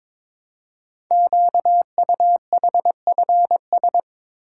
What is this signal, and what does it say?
Morse code, heard as a keyed tone, about 22 wpm, 700 Hz. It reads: QUHFS
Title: Code Groups - 5 Letters - 22wpm